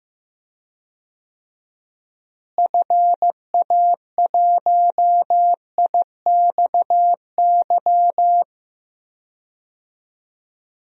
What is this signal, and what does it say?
Morse code, heard as a keyed tone, about 15 wpm, 700 Hz. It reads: FA1IXY
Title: DXCC-100 Call Signs - 6 Characters - Repeated 2x - 15wpm